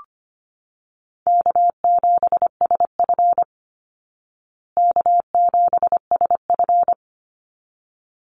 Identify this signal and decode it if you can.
Morse code, heard as a keyed tone, about 25 wpm, 700 Hz. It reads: K7SF K7SF
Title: Easy Call Signs Repeated 2x - 25wpm